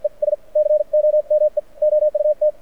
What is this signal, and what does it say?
Morse code, heard as a keyed tone, about 55 wpm, spaced as for 45 wpm, 600 Hz. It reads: ES COME OUT
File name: archived / CRSnet